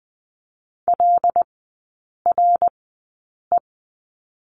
Morse code, heard as a keyed tone, about 20 wpm, spaced as for 8 wpm, 700 Hz. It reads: LRE
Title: 081 Lesson 14e - Number 2 - Three Characters 20wpm